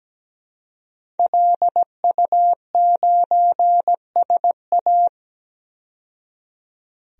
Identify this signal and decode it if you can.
Morse code, heard as a keyed tone, about 17 wpm, 700 Hz. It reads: LU9SA
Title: DXCC-100 Call Signs - 5 Characters - 17wpm